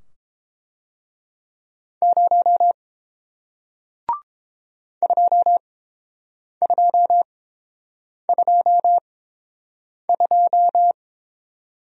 Morse code, heard as a keyed tone, about 29 wpm, spaced as for 7 wpm, 700 Hz. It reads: 0 2222